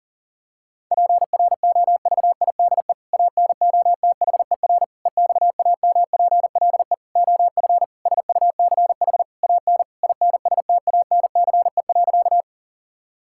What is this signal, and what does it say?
Morse code, heard as a keyed tone, about 37 wpm, 700 Hz. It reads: PROVIDE ANOTHER EXAMPLE OF SUCH AN INSTANCE.
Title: Sentences from Top 1000 Words - Review Part 5 - 40wpm